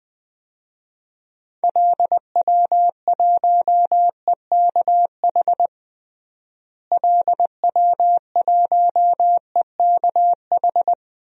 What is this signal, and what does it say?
Morse code, heard as a keyed tone, about 20 wpm, 700 Hz. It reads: LW1EKH LW1EKH
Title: DXCC-100 Call Signs - 6 Characters - Repeated 2x - 20wpm